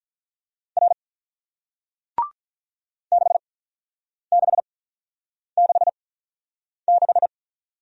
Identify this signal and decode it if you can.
Morse code, heard as a keyed tone, about 47 wpm, spaced as for 9 wpm, 700 Hz. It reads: R 6666